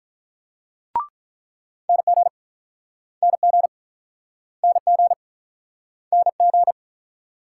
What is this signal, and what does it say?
Morse code, heard as a keyed tone, about 47 wpm, 700 Hz. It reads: NG NG NG NG